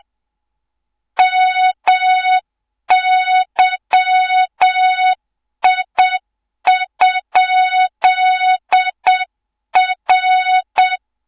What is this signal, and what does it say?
Morse code, heard as a keyed tone, about 7 wpm, 3700 Hz, with noise. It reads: MYI?R
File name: Morze-HA2MN / hang / 24 GYAKORLAT